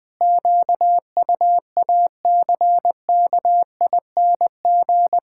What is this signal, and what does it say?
Morse code, heard as a keyed tone, about 20 wpm, 700 Hz. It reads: QUACKING
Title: Pangram Challenge v2 - 20wpm